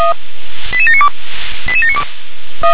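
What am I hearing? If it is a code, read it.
Five-tone selective call (ZVEI), heard as quick runs of five tones, tones of about 80 ms. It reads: 88792 88792
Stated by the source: ZVEI five-tone selective call received via SDR.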